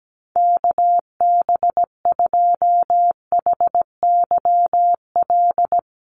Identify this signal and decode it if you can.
Morse code, heard as a keyed tone, about 17 wpm, 700 Hz. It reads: KB2HYL
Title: USA Call Signs Repeated 2x 17wpm